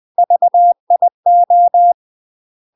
Morse code, heard as a keyed tone, about 20 wpm, 700 Hz. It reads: VIO
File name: UK MASTER QSOs TRAINING / 20WPM / UK MASTER QSOs 0070